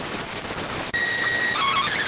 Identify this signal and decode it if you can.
SSTV (slow-scan television), PD 290.